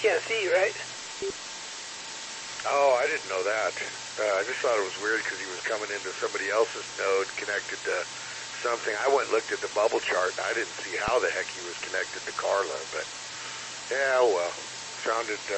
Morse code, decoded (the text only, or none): none